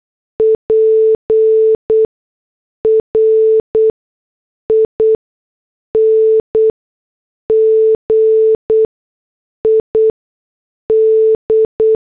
Morse code, heard as a keyed tone, about 8 wpm, 440 Hz. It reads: P R I N G I D
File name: morse / headlines